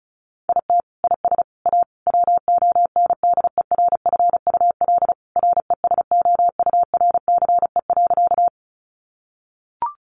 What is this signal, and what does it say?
Morse code, heard as a keyed tone, about 34 wpm, 700 Hz. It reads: IT IS A WONDERFUL RESOURCE.
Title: Sentences from Top 1300 Words 35wpm